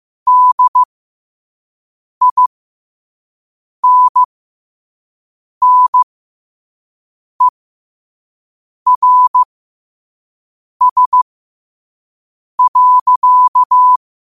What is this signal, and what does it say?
Morse code, heard as a keyed tone, about 15 wpm, spaced as for 5 wpm, 1000 Hz. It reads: DINNERS.
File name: qotdfiles / 5wpm